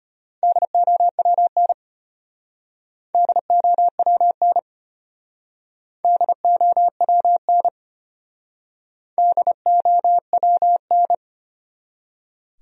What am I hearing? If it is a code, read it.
Morse code, heard as a keyed tone, about 32 wpm, 700 Hz. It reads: DOWN DOWN DOWN DOWN